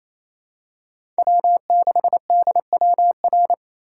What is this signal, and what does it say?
Morse code, heard as a keyed tone, about 28 wpm, 700 Hz. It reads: W6DWR
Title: USA Call Signs - 5 Characters - 4x Character Spacing - 28wpm